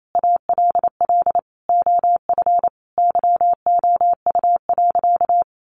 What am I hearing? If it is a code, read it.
Morse code, heard as a keyed tone, about 28 wpm, 700 Hz. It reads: ALL OF YOU.
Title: Sentences from Top 1400 Words 28wpm